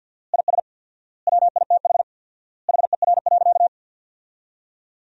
Morse code, heard as a keyed tone, about 50 wpm, 700 Hz. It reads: IS WITH HER.
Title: 4x Word-Spacing - Sentences from Top 100 Words - 50wpm